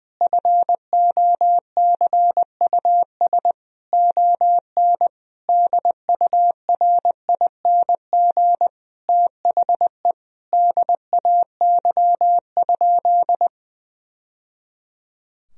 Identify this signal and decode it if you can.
Morse code, heard as a keyed tone, about 20 wpm, 700 Hz. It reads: FOCUS ON DURING THE DAY?
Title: Sentences from Top 1000 Words - Review Part 4 - 20wpm